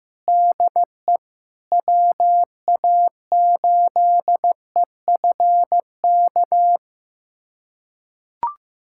Morse code, heard as a keyed tone, about 16 wpm, 700 Hz. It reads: DE WA8EFK
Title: USA Callsign DE Callsign 15wpm